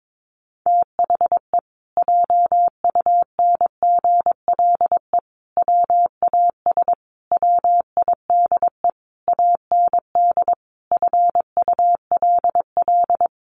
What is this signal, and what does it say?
Morse code, heard as a keyed tone, about 22 wpm, 700 Hz. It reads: THE JUNGLE WAS WIDE AND FULL
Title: A Sound Of Thunder 22wpm